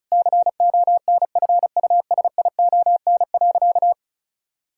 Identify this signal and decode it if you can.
Morse code, heard as a keyed tone, about 34 wpm, 700 Hz. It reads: CONFUSION.
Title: Sentences from Top 1600 Words 35wpm